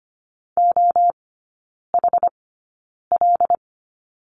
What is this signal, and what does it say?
Morse code, heard as a keyed tone, about 25 wpm, spaced as for 9 wpm, 690 Hz. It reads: OHL
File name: Lesson 14e - Number 2 - Three Characters 25wpm